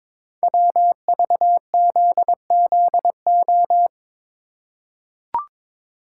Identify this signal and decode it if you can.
Morse code, heard as a keyed tone, about 22 wpm, 700 Hz. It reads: WVZZO
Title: Code Groups - 5 Letters - 22wpm